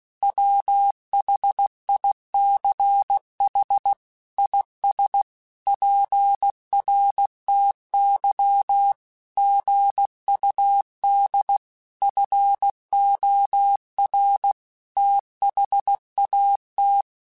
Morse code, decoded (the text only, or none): WHICH IS PRTY GUD FOR THAT